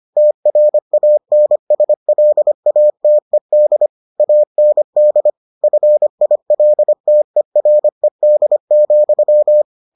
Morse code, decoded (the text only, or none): TRANSLATED AND FILTERED,